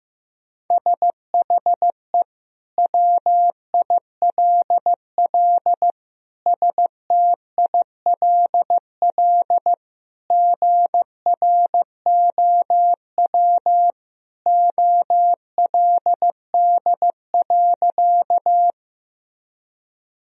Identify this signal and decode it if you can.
Morse code, heard as a keyed tone, about 15 wpm, 700 Hz. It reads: SHE WILL STILL GROW OLD.
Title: Sentences from Top 1000 Words - Review Part 1 - 15wpm